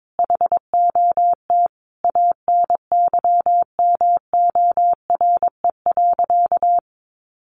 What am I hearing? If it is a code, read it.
Morse code, heard as a keyed tone, about 22 wpm, 700 Hz. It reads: HOT ANYMORE.
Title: Additional Easy Phrases 22wpm